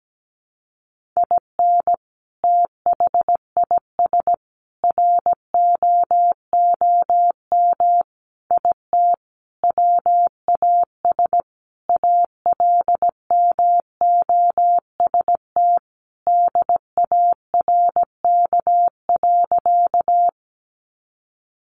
Morse code, heard as a keyed tone, about 17 wpm, 700 Hz. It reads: IN THIS ROOM IT WAS ALMOST DARK.
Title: Sentences from Top 700 Words 17wpm